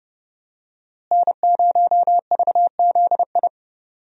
Morse code, heard as a keyed tone, about 31 wpm, 700 Hz. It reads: N0VZI
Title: USA Call Signs - 5 Characters - Repeated 2x - 30wpm